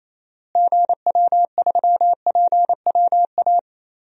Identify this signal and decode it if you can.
Morse code, heard as a keyed tone, about 28 wpm, 700 Hz. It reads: GW3PWA
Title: DXCC-100 Call Signs - 6 Characters - Repeated 3x - 28wpm